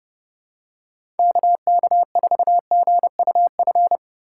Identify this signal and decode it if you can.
Morse code, heard as a keyed tone, about 32 wpm, 690 Hz. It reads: KK4GUF
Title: USA Call Signs - 3x Character Spacing - 30wpm